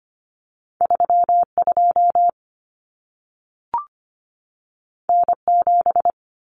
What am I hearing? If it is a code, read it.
Morse code, heard as a keyed tone, about 25 wpm, 700 Hz. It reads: 32 N7
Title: Common Cut Numbers - 2-Digit 25wpm